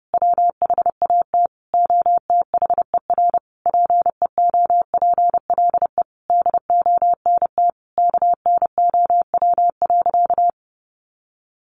Morse code, heard as a keyed tone, about 30 wpm, 700 Hz. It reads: WHAT OTHER PEOPLE DONT KNOW.
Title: Sentences from Top 600 Words 30wpm